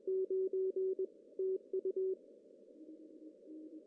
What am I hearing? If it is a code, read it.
Morse code, heard as a keyed tone, about 21 wpm, 380 Hz. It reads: 9 TU
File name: CW / cqww-cw-k5tr